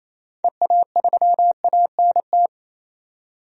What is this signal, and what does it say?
Morse code, heard as a keyed tone, about 28 wpm, 700 Hz. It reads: EA3ANT
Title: DXCC-100 Call Signs - 6 Characters - 28wpm